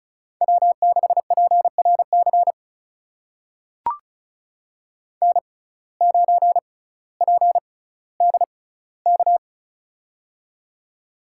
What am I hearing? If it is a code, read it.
Morse code, heard as a keyed tone, about 36 wpm, spaced as for 18 wpm, 700 Hz. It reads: W6PRC N9PDK
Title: USA Call Signs - 5 Characters - 6x Character Spacing - 35wpm